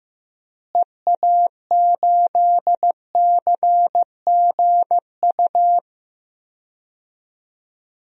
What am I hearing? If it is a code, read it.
Morse code, heard as a keyed tone, about 15 wpm, 700 Hz. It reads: EA8CGU
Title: DXCC-100 Call Signs - 6 Characters - Repeated 2x - 15wpm